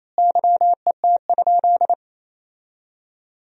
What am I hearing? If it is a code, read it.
Morse code, heard as a keyed tone, about 28 wpm, 700 Hz. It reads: YET?
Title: 4x Word-Spacing - Sentences from Top 500 Words - 28wpm